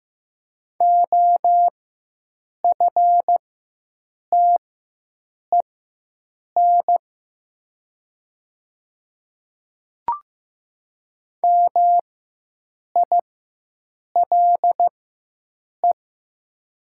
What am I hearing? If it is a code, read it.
Morse code, heard as a keyed tone, about 15 wpm, spaced as for 7 wpm, 700 Hz. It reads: OFTEN MILE
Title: 4x Character Spacing - Top 300 Words - 15wpm